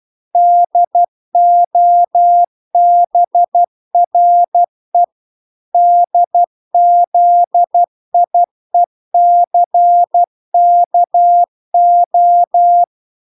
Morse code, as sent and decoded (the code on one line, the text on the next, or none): -.. --- -... .-. / -.. --.. .. -.-. -.- ---
DOBRE DZIECKO